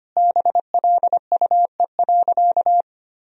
Morse code, heard as a keyed tone, about 25 wpm, 700 Hz. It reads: BLUE.
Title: Sentences from Top 1000 Words - Review Part 3 - 25wpm